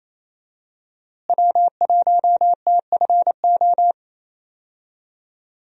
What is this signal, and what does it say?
Morse code, heard as a keyed tone, about 28 wpm, 700 Hz. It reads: W1TFO
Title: USA Call Signs - 5 Characters - Repeated 2x - 28wpm